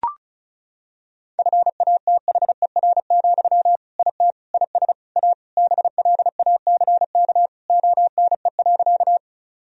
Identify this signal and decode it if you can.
Morse code, heard as a keyed tone, about 36 wpm, 700 Hz. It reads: FATHER, IT IS A BLACK ONE.
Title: Sentences from Top 1000 Words - Review Part 2 - 35wpm